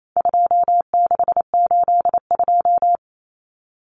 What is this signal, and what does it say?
Morse code, heard as a keyed tone, about 28 wpm, 700 Hz. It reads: 2682
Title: Common Cut Numbers - 4-Digit 28wpm